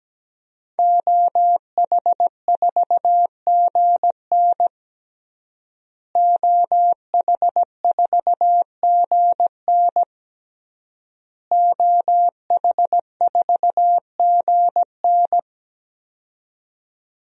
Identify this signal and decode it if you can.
Morse code, heard as a keyed tone, about 17 wpm, 700 Hz. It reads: OH4GN OH4GN OH4GN
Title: DXCC-100 Call Signs - 5 Characters - Repeated 3x - 17wpm